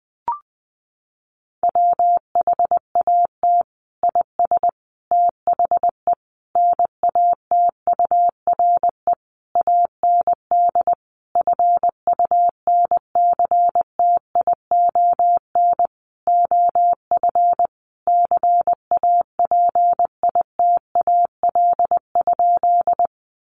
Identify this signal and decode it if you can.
Morse code, heard as a keyed tone, about 20 wpm, 700 Hz. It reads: WHAT IS THE NATURE AND FUNCTION OF CAPITAL?
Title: Sentences from Top 1100 Words 20wpm